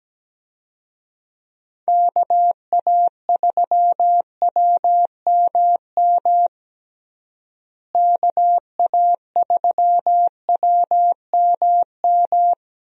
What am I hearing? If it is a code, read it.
Morse code, heard as a keyed tone, about 17 wpm, 700 Hz. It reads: KA3WMM KA3WMM
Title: USA Call Signs - 6 Characters - Repeated 2x - 17wpm